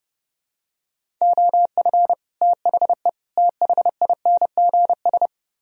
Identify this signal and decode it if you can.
Morse code, heard as a keyed tone, about 29 wpm, 700 Hz. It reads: OF THE THINGS